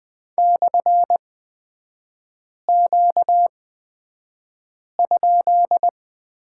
Morse code, heard as a keyed tone, about 20 wpm, spaced as for 5 wpm, 700 Hz. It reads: /Q?
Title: 236 Lesson 38e - Letter q - Three Characters 20wpm